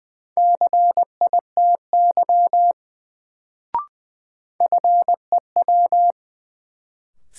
Morse code, heard as a keyed tone, about 20 wpm, 700 Hz. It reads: CITY FEW
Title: Sentences from Top 300 Words - New Word Drill - 20wpm